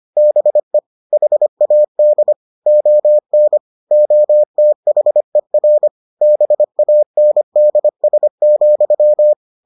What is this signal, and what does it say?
Morse code, heard as a keyed tone, about 25 wpm, 600 Hz. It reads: BE HAD ON OTHER BANDS,